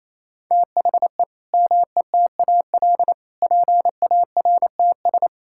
Morse code, heard as a keyed tone, about 28 wpm, 700 Hz. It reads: THE METAL PARTS
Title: Sentences from Top 1000 Words - Review Part 6 - 28wpm